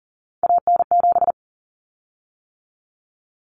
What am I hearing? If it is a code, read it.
Morse code, heard as a keyed tone, about 40 wpm, 700 Hz. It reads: AN7
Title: Common Cut Numbers - 3-Digit 40wpm